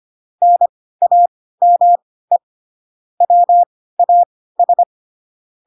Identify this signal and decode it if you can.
Morse code, heard as a keyed tone, about 25 wpm, spaced as for 16 wpm, 700 Hz. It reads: NAME WAS
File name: Pinocchio / Pinocchio ch2 10 25 ik1hge